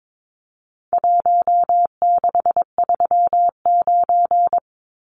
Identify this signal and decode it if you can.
Morse code, heard as a keyed tone, about 22 wpm, 700 Hz. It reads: 1639
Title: Numbers - Contest Style - 4-Digit 22wpm